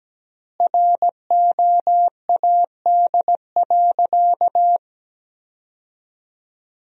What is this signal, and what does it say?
Morse code, heard as a keyed tone, about 17 wpm, 700 Hz. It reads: ROAD.
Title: 3x Word-Spacing - Sentences from Top 400 Words - 17wpm